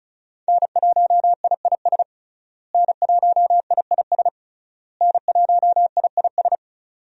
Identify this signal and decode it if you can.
Morse code, heard as a keyed tone, about 34 wpm, 700 Hz. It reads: N1IIS N1IIS N1IIS
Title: USA Call Signs - 5 Characters - Repeated 3x - 35wpm